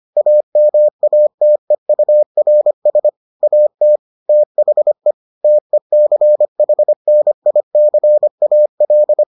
Morse code, heard as a keyed tone, about 25 wpm, 600 Hz. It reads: AMATEURS AT THE TECHNICAL